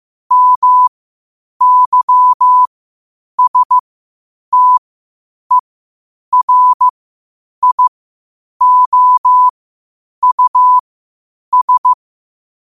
Morse code, heard as a keyed tone, about 15 wpm, spaced as for 8 wpm, 1000 Hz. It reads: MYSTERIOUS
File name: qotdfiles / 8wpm